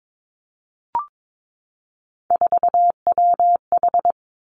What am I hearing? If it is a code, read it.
Morse code, heard as a keyed tone, about 22 wpm, 700 Hz. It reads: 4WH
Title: Three Challenging Letter-Number Combinations 22wpm